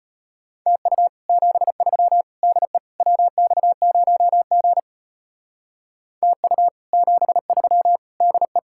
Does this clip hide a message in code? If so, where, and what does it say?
Yes: Morse code from 0.7 to 8.6 s, heard as a keyed tone, about 32 wpm, 700 Hz. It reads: TU 73 DE WX0G TU 73 DE